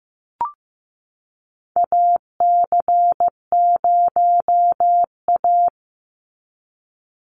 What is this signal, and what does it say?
Morse code, heard as a keyed tone, about 16 wpm, 700 Hz. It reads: AC0A
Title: NA SOTA Chaser Callsigns 15wpm